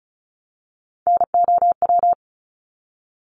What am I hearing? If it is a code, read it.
Morse code, heard as a keyed tone, about 34 wpm, 700 Hz. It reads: NOW
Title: Top 300 Words Repeated 35wpm